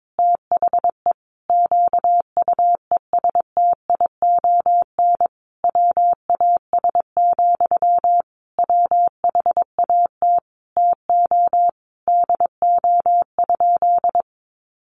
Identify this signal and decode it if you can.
Morse code, heard as a keyed tone, about 22 wpm, 700 Hz. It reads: THE QUESTION WAS, WHAT TO DO?
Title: Sentences from Top 400 Words 22wpm